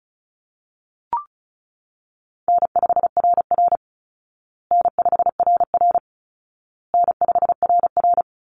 Morse code, heard as a keyed tone, about 34 wpm, 700 Hz. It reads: N5RR N5RR N5RR
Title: Easy Call Signs Repeated 3x - 35wpm